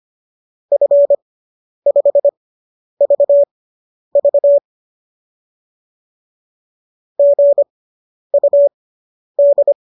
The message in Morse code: F5VV GUD

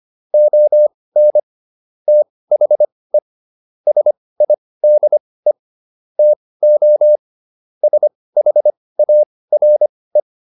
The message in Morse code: ON THE SIDE TO SHARE

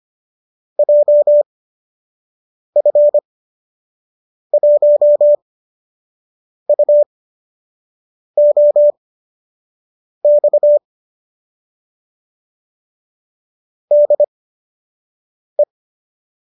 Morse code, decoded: JF1UOX DE

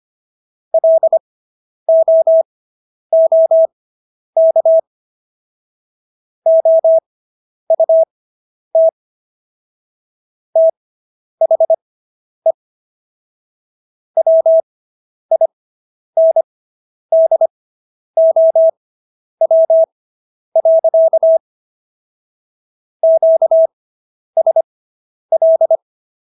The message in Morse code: LOOK OUT THE WINDOW. QSL

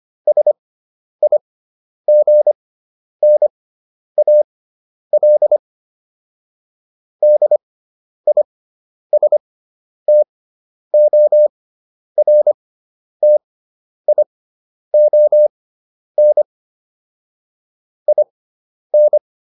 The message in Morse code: SIGNAL DISTORTION IN

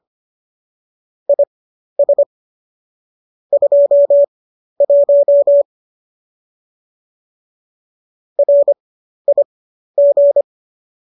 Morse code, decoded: IS 21 RIG

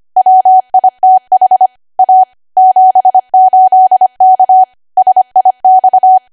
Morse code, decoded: WITH A 78K SIX

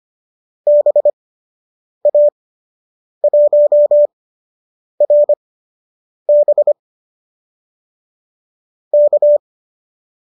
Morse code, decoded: BA1RB K

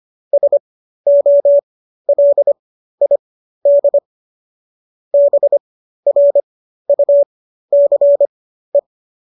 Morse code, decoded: SOLID BRUCE

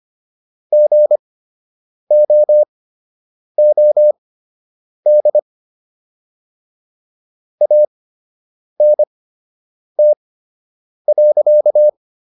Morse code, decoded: GOOD ANT.

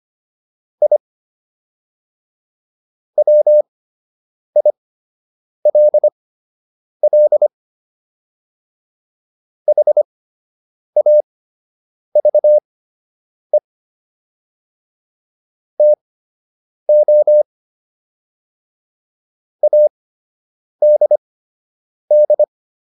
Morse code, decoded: I WILL HAVE TO ADD

